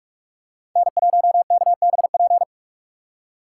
Morse code, heard as a keyed tone, about 41 wpm, 700 Hz. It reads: N1KBP
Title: USA Call Signs - 5 Characters - 4x Character Spacing - 45wpm